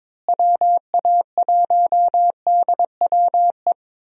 Morse code, heard as a keyed tone, about 22 wpm, 700 Hz. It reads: WA1DWE
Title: USA Call Signs - 6 Characters - Repeated 3x - 22wpm